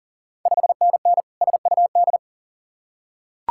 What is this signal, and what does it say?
Morse code, heard as a keyed tone, about 40 wpm, 690 Hz. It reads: 5NN SUD
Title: Ontario QSO Party Exchange 40wpm